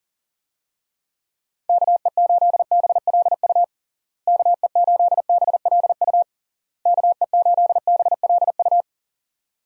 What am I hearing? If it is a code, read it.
Morse code, heard as a keyed tone, about 37 wpm, 700 Hz. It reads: KE8BLU KE8BLU KE8BLU
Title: USA Call Signs - 6 Characters - Repeated 3x - 40wpm